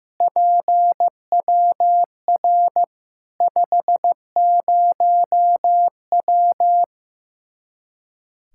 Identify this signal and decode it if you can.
Morse code, heard as a keyed tone, about 15 wpm, 700 Hz. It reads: PWR 50W
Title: Practice QSO - Exchange 2 - Info - 15wpm